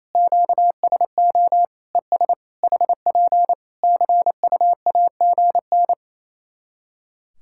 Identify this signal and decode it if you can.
Morse code, heard as a keyed tone, about 28 wpm, 700 Hz. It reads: QSO ES HP CUAGN
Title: Practice QSO - Exchange 4 - Ending - 28wpm